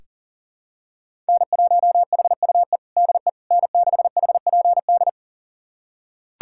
Morse code, heard as a keyed tone, about 40 wpm, 700 Hz. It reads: N1HUE DE N6HPD
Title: USA Callsign DE Callsign 40wpm